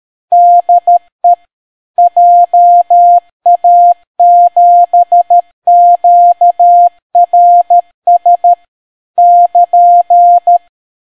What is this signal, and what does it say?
Morse code, heard as a keyed tone, about 13 wpm, 700 Hz. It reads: DE JA7QRS <KN>